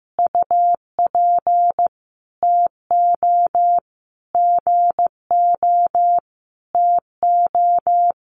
Morse code, heard as a keyed tone, about 15 wpm, 700 Hz. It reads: UP TO GO TO
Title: Sentences from Top 1500 Words 15wpm